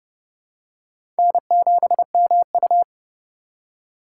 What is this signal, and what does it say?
Morse code, heard as a keyed tone, about 30 wpm, 700 Hz. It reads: N7MU
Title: USA Call Signs - 4x Character Spacing - 30wpm